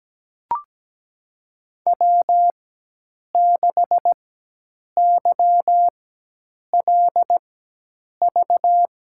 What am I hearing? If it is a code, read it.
Morse code, heard as a keyed tone, about 17 wpm, spaced as for 8 wpm, 700 Hz. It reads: W6YLV
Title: USA Call Signs - 5 Characters - 4x Character Spacing - 17wpm